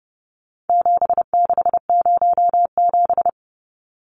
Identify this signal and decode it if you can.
Morse code, heard as a keyed tone, about 30 wpm, 700 Hz. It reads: 7607
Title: Numbers - 4-Digit 30wpm